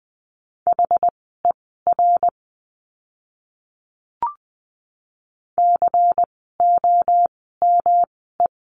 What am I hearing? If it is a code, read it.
Morse code, heard as a keyed tone, about 20 wpm, spaced as for 15 wpm, 700 Hz. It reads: HER COME
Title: 160 Lesson 26f - Letter g - Words 20wpm